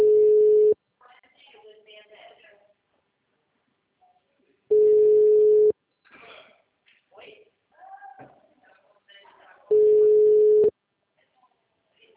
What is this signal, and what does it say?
Ringback tone (Europe, 425 Hz).